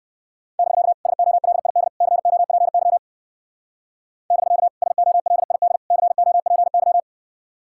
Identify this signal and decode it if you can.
Morse code, heard as a keyed tone, about 55 wpm, 700 Hz. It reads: <BK> SOLID COPY <BK> SOLID COPY
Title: Speed-Racing Common QSO Phrases - 45wpm